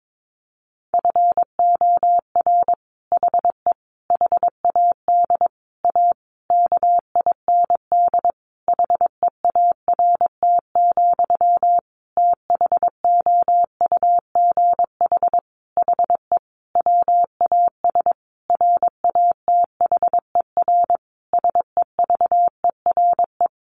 Morse code, heard as a keyed tone, about 22 wpm, 700 Hz. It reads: FOR HE HAD A KIND HEART, THOUGH HE WAS RATHER SEVERE
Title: Sentences from Top 1400 Words 22wpm